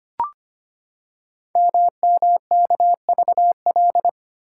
Morse code, heard as a keyed tone, about 25 wpm, 700 Hz. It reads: MMKVL
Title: Code Groups - 5 Letters - 25wpm